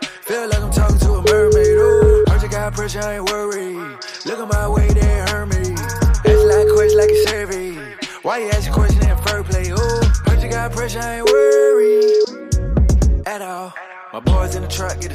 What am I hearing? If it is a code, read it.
Ringback tone (Europe, 425 Hz), with noise.